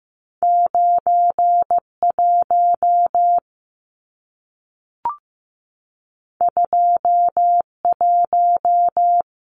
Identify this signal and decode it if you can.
Morse code, heard as a keyed tone, about 15 wpm, 700 Hz. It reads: 9 1 2 1
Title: Numbers - 2-Digit 15wpm